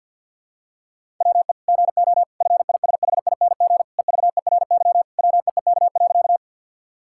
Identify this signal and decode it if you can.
Morse code, heard as a keyed tone, about 50 wpm, 700 Hz. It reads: WE GO FISHING EVERY WEEK.
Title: Sentences from Top 1000 Words - Review Part 5 - 50wpm